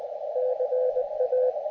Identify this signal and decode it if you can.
Morse code, heard as a keyed tone, about 20 wpm, 510 Hz, with radio noise. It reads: CA
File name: Expedition / SWL UT5NM 20.12.2012 VANGUARD UA1CEG p IT9IFI